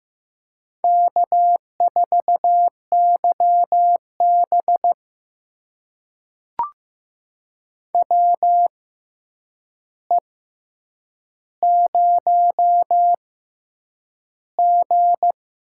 Morse code, heard as a keyed tone, about 15 wpm, spaced as for 8 wpm, 700 Hz. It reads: K4YB WE0G
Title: USA Call Signs - 4 Characters - 6x Character Spacing - 15wpm